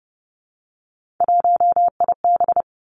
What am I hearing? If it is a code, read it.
Morse code, heard as a keyed tone, about 30 wpm, 700 Hz. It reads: 1IB
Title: Three Challenging Letter-Number Combinations 30wpm